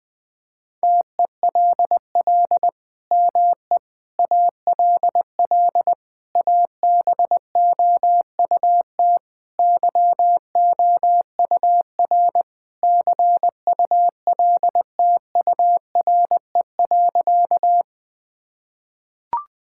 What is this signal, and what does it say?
Morse code, heard as a keyed tone, about 21 wpm, 700 Hz. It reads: TELL ME ALL ABOUT YOUR CULTURE.
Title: Sentences from Top 1000 Words - Review Part 6 - 20wpm